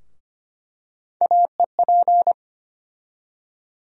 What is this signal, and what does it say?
Morse code, heard as a keyed tone, about 25 wpm, 700 Hz. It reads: AEP
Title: USA Call Sign Postfixes - 3 Letters 25wpm